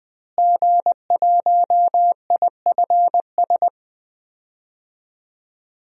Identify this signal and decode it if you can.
Morse code, heard as a keyed tone, about 20 wpm, 700 Hz. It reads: G1IFS
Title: DXCC-100 Call Signs - 5 Characters - Repeated 3x - 20wpm